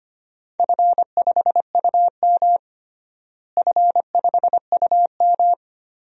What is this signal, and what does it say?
Morse code, heard as a keyed tone, about 25 wpm, 700 Hz. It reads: F5UM F5UM
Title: DXCC-100 Call Signs - 4 Characters - Repeated 3x - 25wpm